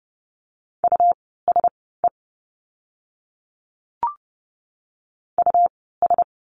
Morse code, heard as a keyed tone, about 30 wpm, spaced as for 17 wpm, 700 Hz. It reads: USE US
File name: course / 30:10wpm / 115 Lesson 19f - Letter w - Words 30wpm